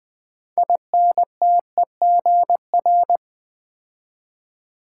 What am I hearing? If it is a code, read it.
Morse code, heard as a keyed tone, about 20 wpm, 700 Hz. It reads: INTEGR
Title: Build A Word - 3 Letters At A Time - 9-Letter Words - 20wpm